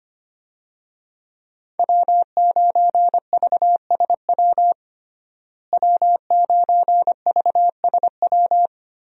Morse code, heard as a keyed tone, about 25 wpm, 700 Hz. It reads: W9VSW W9VSW
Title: USA Call Signs - 5 Characters - Repeated 3x - 25wpm